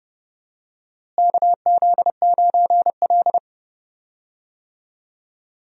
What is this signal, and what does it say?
Morse code, heard as a keyed tone, about 30 wpm, 700 Hz. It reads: KZ9L
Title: USA Call Signs - 4 Characters - Repeated 3x - 30wpm